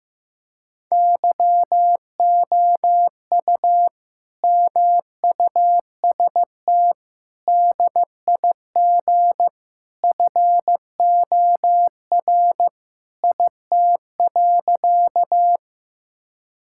Morse code, heard as a keyed tone, about 16 wpm, 700 Hz. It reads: YOU MUST DIG FOR IT.
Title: Sentences from Top 1600 Words 15wpm